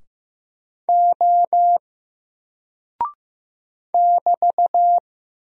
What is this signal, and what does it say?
Morse code, heard as a keyed tone, about 15 wpm, 700 Hz. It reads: O <BT>
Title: Single Character (Letter:Number:Punctuation:Prosign) Alt 15wpm